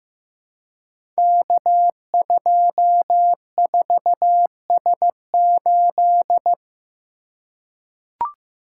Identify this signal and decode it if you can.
Morse code, heard as a keyed tone, about 15 wpm, 700 Hz. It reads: K24S8
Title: Code Groups - 5 Letter:Numbers - 15wpm